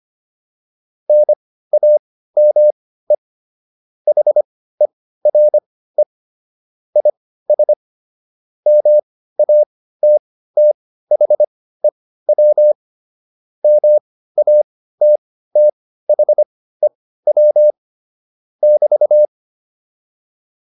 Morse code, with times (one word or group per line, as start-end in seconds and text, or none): 1.1-3.2 NAME
4.1-6.0 HERE
7.0-7.7 IS
8.7-12.7 MATTHEW
13.6-17.7 MATTHEW
18.6-19.3 <BT>